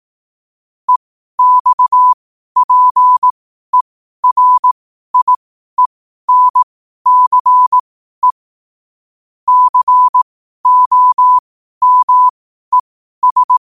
Morse code, heard as a keyed tone, about 18 wpm, spaced as for 12 wpm, 1000 Hz. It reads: EXPERIENCE COMES
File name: qotdfiles / 12wpm